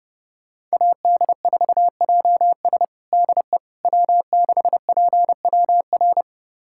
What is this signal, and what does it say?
Morse code, heard as a keyed tone, about 32 wpm, 700 Hz. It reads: AD4JS DE W6PWR
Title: USA Callsign DE Callsign - 5 Characters - 30wpm